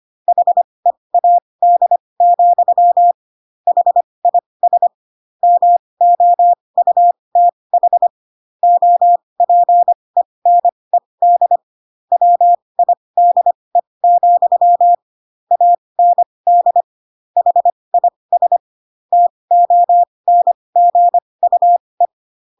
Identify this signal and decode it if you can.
Morse code, heard as a keyed tone, about 25 wpm, spaced as for 20 wpm, 700 Hz. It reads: HEAD, HIS MOUTH OPENED WIDE, AND HIS TONGUE
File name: Pinocchio / Pinocchio ch1 15 25 ik1hge